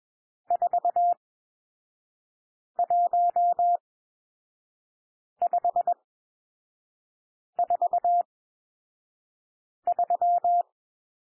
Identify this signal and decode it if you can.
Morse code, heard as a keyed tone, about 21 wpm, spaced as for 5 wpm, 700 Hz. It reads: 41543
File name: cwmp3 / N1 SOUND NUMBERS 21 wpm